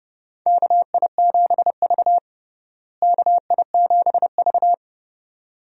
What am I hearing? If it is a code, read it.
Morse code, heard as a keyed tone, about 30 wpm, 700 Hz. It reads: KI7V KI7V
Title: USA Call Signs - 4 Characters - Repeated 2x - 30wpm